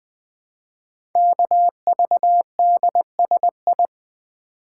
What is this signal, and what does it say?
Morse code, heard as a keyed tone, about 20 wpm, 700 Hz. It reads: KVDSI
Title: Code Groups - 5 Letters - 20wpm